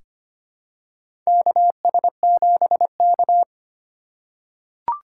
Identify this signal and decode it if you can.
Morse code, heard as a keyed tone, about 25 wpm, 700 Hz. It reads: KS7K
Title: USA Call Signs - 4 Characters - 4x Character Spacing - 25wpm